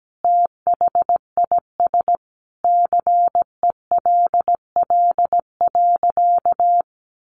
Morse code, heard as a keyed tone, about 17 wpm, 700 Hz. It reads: THIS CELL.
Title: Sentences from Top 1100 Words 17wpm